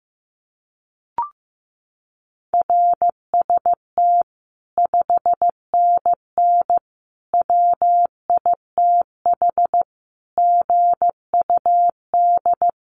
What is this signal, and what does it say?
Morse code, heard as a keyed tone, about 15 wpm, 700 Hz. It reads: RST 5NN WITH GUD
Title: Common QSO Phrases 15wpm